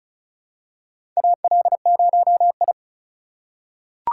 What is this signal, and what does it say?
Morse code, heard as a keyed tone, about 36 wpm, 700 Hz. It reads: AL0I
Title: USA Call Signs - 4 Characters - 3x Character Spacing - 35wpm